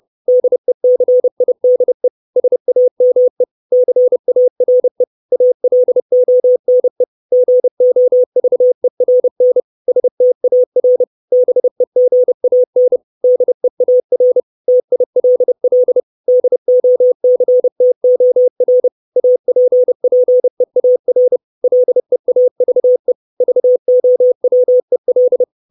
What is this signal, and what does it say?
Morse code, heard as a keyed tone, about 30 wpm, 500 Hz. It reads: DECIDE SAME CARE ALONE GOVERN STAR BEGAN DEAR TILL DOCTOR APPEAR LEAVE VOWEL
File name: 632 Words 30WPM